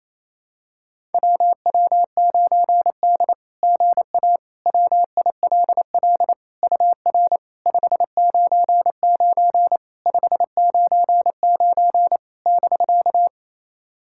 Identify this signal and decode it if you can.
Morse code, heard as a keyed tone, about 28 wpm, 690 Hz. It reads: WW9D GA WILL UR 599 599 <BK>
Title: SOTA QSO Practice 28wpm